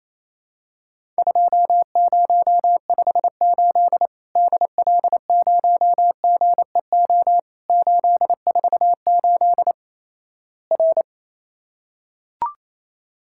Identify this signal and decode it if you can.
Morse code, heard as a keyed tone, about 28 wpm, 700 Hz. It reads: 2058 DL0GEO 848 R
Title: Worked All Europe DX Contest - QTC Exchange - 28wpm